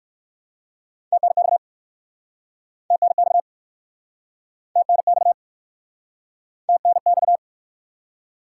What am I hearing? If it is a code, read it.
Morse code, heard as a keyed tone, about 59 wpm, 700 Hz. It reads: TNX TNX TNX TNX